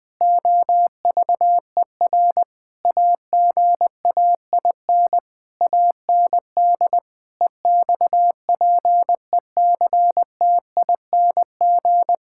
Morse code, heard as a keyed tone, about 20 wpm, 700 Hz. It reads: OVER AGAIN AND EXPECTING
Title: Sentences with 8-Word Phrases - Spoken Beginning - 20wpm